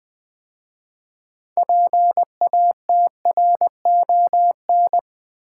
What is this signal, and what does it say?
Morse code, heard as a keyed tone, about 20 wpm, 700 Hz. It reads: PATRON
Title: 6 Letter Words - First 3 Letters Given - 20wpm